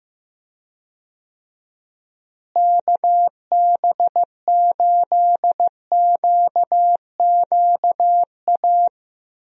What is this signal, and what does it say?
Morse code, heard as a keyed tone, about 15 wpm, 700 Hz. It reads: KB8QQA
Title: USA Call Signs - 6 Characters - Repeated 2x - 15wpm